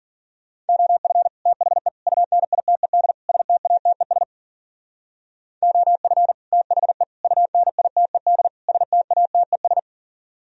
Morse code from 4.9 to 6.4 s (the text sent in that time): OF